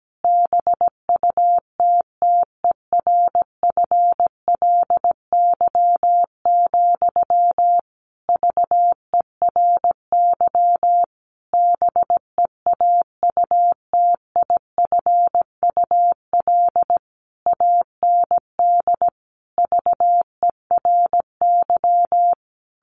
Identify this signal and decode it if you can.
Morse code, heard as a keyed tone, about 17 wpm, 700 Hz. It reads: BUTTERFLY, VERY BEAUTIFUL AND VERY
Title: A Sound Of Thunder 17wpm